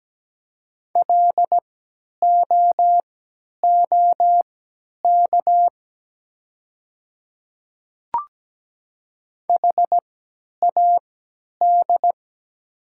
Morse code, heard as a keyed tone, about 17 wpm, spaced as for 10 wpm, 690 Hz. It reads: LOOK HAD
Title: 3x Character Spacing - Top 100 Words - 17wpm